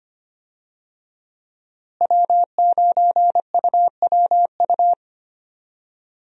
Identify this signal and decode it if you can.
Morse code, heard as a keyed tone, about 25 wpm, 700 Hz. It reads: W9UWU
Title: USA Call Signs - 5 Characters - Repeated 2x - 25wpm